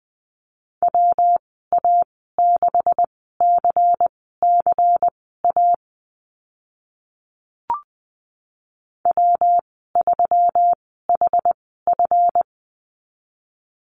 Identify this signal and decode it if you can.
Morse code, heard as a keyed tone, about 20 wpm, spaced as for 14 wpm, 700 Hz. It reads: WA6CCA W3HF
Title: 140 Lesson 23g - Letter f - Call Signs 20wpm